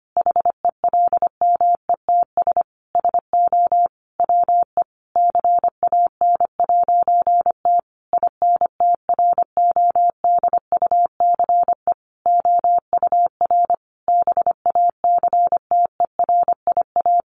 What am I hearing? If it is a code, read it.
Morse code, heard as a keyed tone, about 25 wpm, 700 Hz. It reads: HELMETS SO WE CAN'T INTRODUCE OUR BACTERIA
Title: A Sound Of Thunder 25wpm